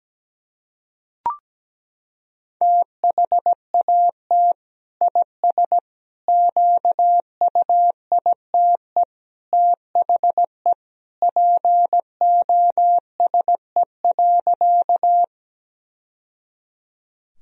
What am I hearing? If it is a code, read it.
Morse code, heard as a keyed tone, about 17 wpm, 700 Hz. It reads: THAT IS QUITE THE POSE.
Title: Sentences from Top 1000 Words - Review Part 2 - 17wpm